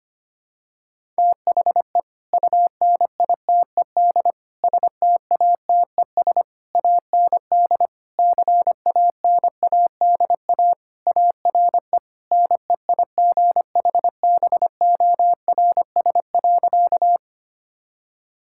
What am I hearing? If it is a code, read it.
Morse code, heard as a keyed tone, about 25 wpm, 700 Hz. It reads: THE UNITED STATES AND CANADA ARE NEIGHBORS.
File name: practice / 25wpm / Sentences with 3-Word Phrases - 25wpm